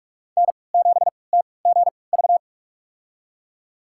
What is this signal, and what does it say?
Morse code, heard as a keyed tone, about 44 wpm, spaced as for 25 wpm, 700 Hz. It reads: N7TGV
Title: USA Call Signs - 5 Characters - 3x Character Spacing - 45wpm